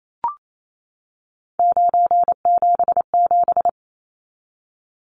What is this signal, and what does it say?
Morse code, heard as a keyed tone, about 28 wpm, 700 Hz. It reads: 977
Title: Numbers - 3-Digit 28wpm